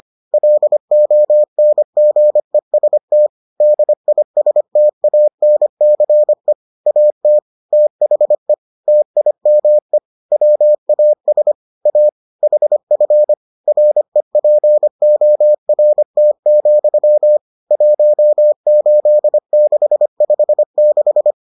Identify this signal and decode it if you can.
Morse code, heard as a keyed tone, about 25 wpm, 600 Hz. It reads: LONGEST DISTANCE AT THE TIME WAS A HF REPORT, 18656